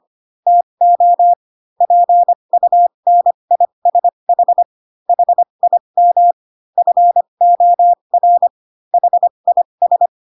Morse code, heard as a keyed tone, about 25 wpm, spaced as for 22 wpm, 700 Hz. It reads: TO PUNISH HIM FOR HIS
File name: Pinocchio / Pinocchio ch8 18 25 ik1hge